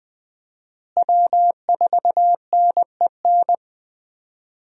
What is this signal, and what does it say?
Morse code, heard as a keyed tone, about 20 wpm, 700 Hz. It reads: W4NEN
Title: USA Call Signs - 5 Characters - Repeated 2x - 20wpm